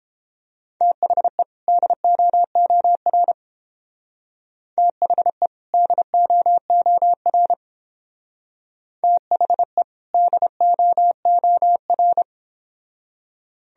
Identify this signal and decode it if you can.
Morse code, heard as a keyed tone, about 30 wpm, 700 Hz. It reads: THE DOOR THE DOOR THE DOOR